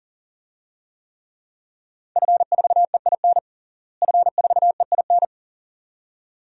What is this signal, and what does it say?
Morse code, heard as a keyed tone, about 37 wpm, 700 Hz. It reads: F4EIN F4EIN
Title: DXCC-100 Call Signs - 5 Characters - Repeated 2x - 40wpm